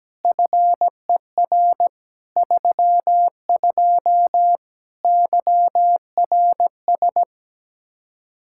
Morse code, heard as a keyed tone, about 17 wpm, 700 Hz. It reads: FER 32 YRS
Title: K1USN SST QSO Practice 17wpm